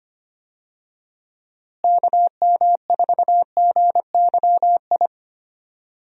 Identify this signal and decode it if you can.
Morse code, heard as a keyed tone, about 25 wpm, 700 Hz. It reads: KM4GYI
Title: USA Call Signs - 6 Characters - Repeated 2x - 25wpm